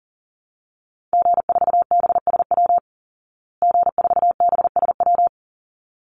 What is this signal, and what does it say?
Morse code, heard as a keyed tone, about 40 wpm, 700 Hz. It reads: G4BSW G4BSW
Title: Call Signs Repeated 3x - 40wpm